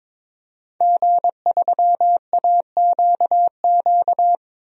Morse code, heard as a keyed tone, about 22 wpm, 700 Hz. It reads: G3AQQ
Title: DXCC-100 Call Signs - 5 Characters - 22wpm